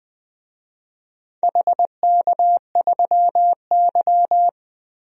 Morse code, heard as a keyed tone, about 20 wpm, 700 Hz. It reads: HK3Y
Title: DXCC-100 Call Signs - 4 Characters - 20wpm